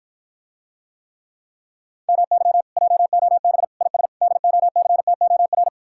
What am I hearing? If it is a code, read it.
Morse code, heard as a keyed tone, about 54 wpm, 700 Hz. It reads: MY JOB IS DOCTOR